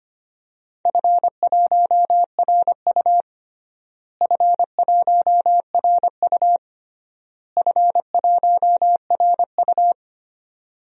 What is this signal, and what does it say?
Morse code, heard as a keyed tone, about 25 wpm, 700 Hz. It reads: F1RU F1RU F1RU
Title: DXCC-100 Call Signs - 4 Characters - Repeated 3x - 25wpm